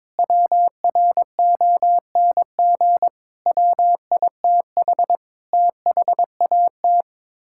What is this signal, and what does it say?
Morse code, heard as a keyed tone, about 22 wpm, 700 Hz. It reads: WRONG WITH THAT
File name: practice / 22wpm / Sets of 7 Words - 22wpm